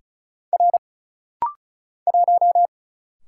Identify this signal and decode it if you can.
Morse code, heard as a keyed tone, about 36 wpm, 700 Hz. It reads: R 1
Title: Single Letter-Number - Rapid-Fire 35wpm